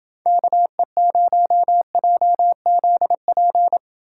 Morse code, heard as a keyed tone, about 29 wpm, 700 Hz. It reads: KE0JZP